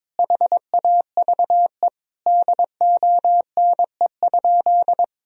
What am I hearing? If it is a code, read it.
Morse code, heard as a keyed tone, about 21 wpm, 700 Hz. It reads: HAVE DONE?
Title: Sentences from Top 1000 Words - Review Part 3 - 22wpm